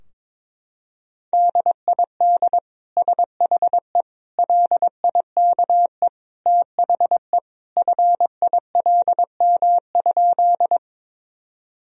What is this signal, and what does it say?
Morse code, heard as a keyed tone, about 22 wpm, 700 Hz. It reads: DID SHE LIKE THE FILM?
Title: Sentences from Top 1000 Words - Review Part 3 - 22wpm